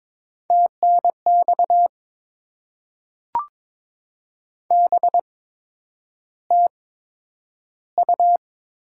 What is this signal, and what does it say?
Morse code, heard as a keyed tone, about 22 wpm, spaced as for 9 wpm, 700 Hz. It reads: TNX BTU
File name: Top 20 QSO Elements - 8x Character Spacing - 22wpm